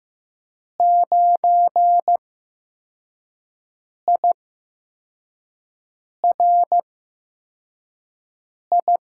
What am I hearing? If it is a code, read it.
Morse code, heard as a keyed tone, about 15 wpm, spaced as for 4 wpm, 700 Hz. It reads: 9IRI
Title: USA Call Signs - 6 Characters - 8x Character Spacing - 15wpm